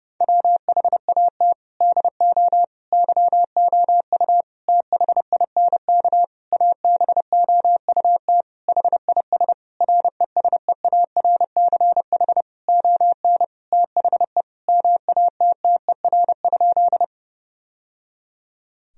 Morse code, heard as a keyed tone, about 32 wpm, 700 Hz. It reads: WHAT DO YOU THINK ABOUT HIS RESEARCH ON THE MATTER?
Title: Sentences from Top 1000 Words - Review Part 4 - 30wpm